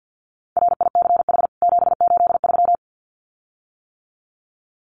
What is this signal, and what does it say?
Morse code, heard as a keyed tone, about 48 wpm, 700 Hz. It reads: RICH 783
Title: CWT Exchange 50wpm